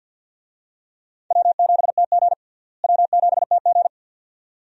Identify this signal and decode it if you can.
Morse code, heard as a keyed tone, about 53 wpm, 700 Hz. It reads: W7TG W7TG
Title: USA Call Signs - 4 Characters - Repeated 2x - 50wpm